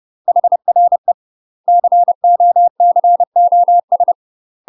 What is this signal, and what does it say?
Morse code, heard as a keyed tone, about 30 wpm, 700 Hz. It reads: HRE COCOS
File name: UK MASTER QSOs TRAINING / 30WPM / UK MASTER QSOs 0027